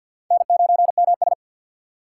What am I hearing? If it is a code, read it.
Morse code, heard as a keyed tone, about 53 wpm, 700 Hz. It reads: N9MS
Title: K1USN SST QSO Practice - 50wpm